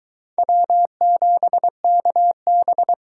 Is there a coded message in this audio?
Yes: Morse code from 0.4 to 2.9 s, heard as a keyed tone, about 23 wpm, 700 Hz. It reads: W7KB